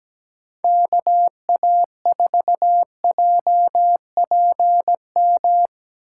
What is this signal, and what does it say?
Morse code, heard as a keyed tone, about 19 wpm, 700 Hz. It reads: KA4JPM
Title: Speed Racing - USA 2x3 Call Signs 17wpm